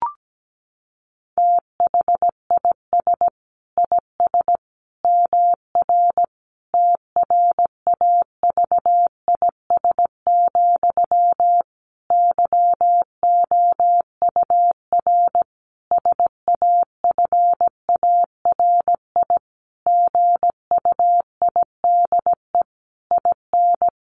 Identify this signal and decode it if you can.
Morse code, heard as a keyed tone, about 17 wpm, 700 Hz. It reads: THIS IS MR TRAVIS, YOUR SAFARI GUIDE IN
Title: A Sound Of Thunder 17wpm